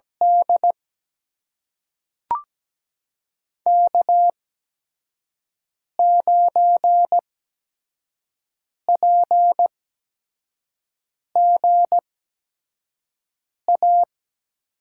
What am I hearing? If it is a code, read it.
Morse code, heard as a keyed tone, about 17 wpm, spaced as for 5 wpm, 700 Hz. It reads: D K9PGA
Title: USA Call Signs - 5 Characters - 8x Character Spacing - 17wpm